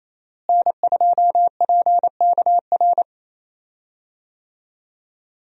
Morse code, heard as a keyed tone, about 28 wpm, 700 Hz. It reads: N2PKR
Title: USA Call Signs - 5 Characters - Repeated 3x - 28wpm